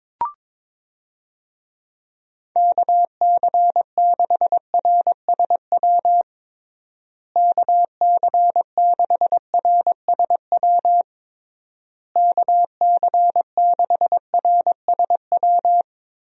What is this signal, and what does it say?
Morse code, heard as a keyed tone, about 22 wpm, 700 Hz. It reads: KC6RSW KC6RSW KC6RSW
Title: USA Call Signs - 6 Characters - Repeated 3x - 22wpm